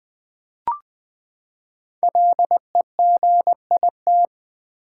Morse code, heard as a keyed tone, about 20 wpm, 700 Hz. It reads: LEGIT
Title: Build A Word - 5 Letters At A Time - 10-Letter Words - 20wpm